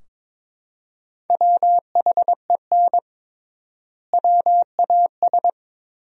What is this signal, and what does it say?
Morse code, heard as a keyed tone, about 22 wpm, 700 Hz. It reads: WHEN WAS
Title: 3x Word-Spacing - Sentences from Top 100 Words - 22wpm